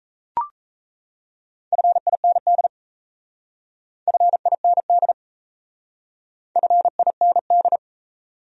Morse code, heard as a keyed tone, about 34 wpm, 700 Hz. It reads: FIND FIND FIND
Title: Speed-Racing Top 100 Words 28wpm